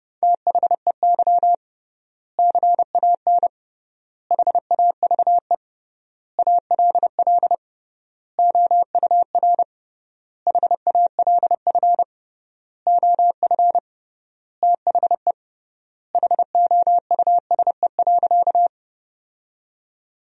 Morse code, heard as a keyed tone, about 30 wpm, 700 Hz. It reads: THEY CAN HAVE ALL OUR HALF OF THE HOUSE.
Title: 3x Word-Spacing - Sentences from Top 400 Words - 30wpm